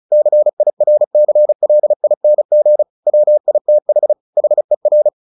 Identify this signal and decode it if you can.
Morse code, heard as a keyed tone, about 34 wpm, 600 Hz. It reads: CIRCLING WITH HER
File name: the old man and the sea / the old man and the sea 1 35wpm